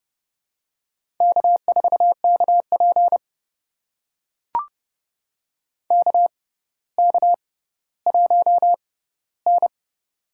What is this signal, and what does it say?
Morse code, heard as a keyed tone, about 29 wpm, spaced as for 15 wpm, 700 Hz. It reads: K4KP KK1N
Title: USA Call Signs - 4 Characters - 6x Character Spacing - 30wpm